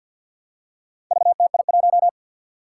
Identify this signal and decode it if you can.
Morse code, heard as a keyed tone, about 50 wpm, 700 Hz. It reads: VTI1
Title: Four Challenging Letter-Number Combinations 50wpm